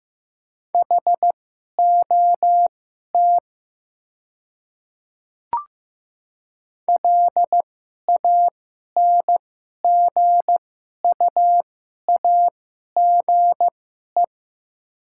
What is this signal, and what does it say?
Morse code, heard as a keyed tone, about 15 wpm, spaced as for 11 wpm, 700 Hz. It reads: HOT LANGUAGE
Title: 2x Character Spacing - Top 500 Words - 15wpm